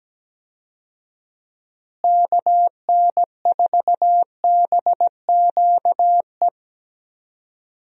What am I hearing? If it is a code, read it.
Morse code, heard as a keyed tone, about 17 wpm, 700 Hz. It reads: KN4BQE
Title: USA Call Signs - 6 Characters - Repeated 2x - 17wpm